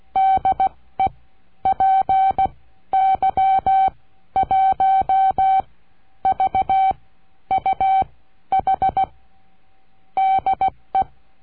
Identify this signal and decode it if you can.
Morse code, heard as a keyed tone, about 16 wpm, spaced as for 11 wpm, 770 Hz. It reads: DE PY1VUH DE
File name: cursos / CURSO DE TELEGRAFIA / Telegrafia - Trilha 22